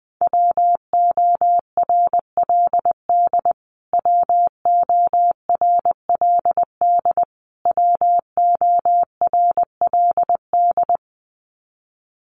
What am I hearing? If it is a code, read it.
Morse code, heard as a keyed tone, about 20 wpm, 700 Hz. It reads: WORLD WORLD WORLD
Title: Top 500 Words Repeated 20wpm